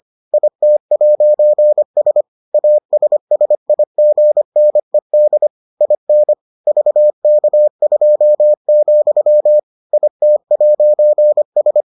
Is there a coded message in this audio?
Yes: Morse code from 0.3 to 11.8 s, heard as a keyed tone, about 25 wpm, 600 Hz. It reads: IT'S ASSIGNED IN VK2, IT'S